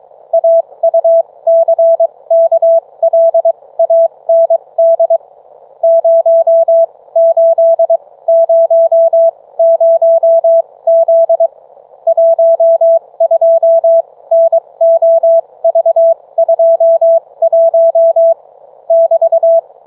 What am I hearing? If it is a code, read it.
Morse code, heard as a keyed tone, about 23 wpm, spaced as for 18 wpm, 660 Hz. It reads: AUCKLAND 0800Z 12NOV21 <BT>